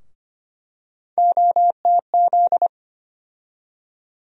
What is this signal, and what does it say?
Morse code, heard as a keyed tone, about 25 wpm, 700 Hz. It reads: OTZ
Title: USA Call Sign Postfixes - 3 Letters 25wpm